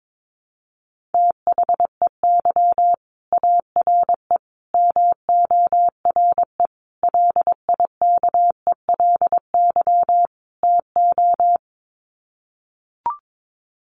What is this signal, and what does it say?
Morse code, heard as a keyed tone, about 22 wpm, 700 Hz. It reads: THEY ARE MORE LIKELY TO
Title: Sets of 5 Words Encore2 22wpm